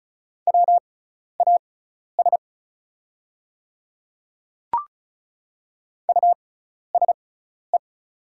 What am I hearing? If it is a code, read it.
Morse code, heard as a keyed tone, about 36 wpm, spaced as for 12 wpm, 690 Hz. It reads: WAS USE
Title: 6x Character Spacing - Top 100 Words - 35wpm